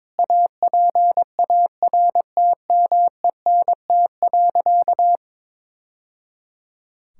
Morse code, heard as a keyed tone, about 22 wpm, 700 Hz. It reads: APARTMENT.
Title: Sentences from Top 1000 Words - Review Part 6 - 22wpm